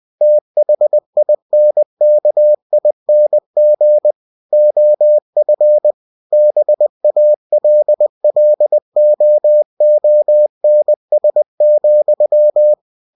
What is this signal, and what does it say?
Morse code, heard as a keyed tone, about 20 wpm, 600 Hz. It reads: THINKING OF BALLOONS,
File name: cw-books / winnie-the-pooh / winnie-20wpm-page-0005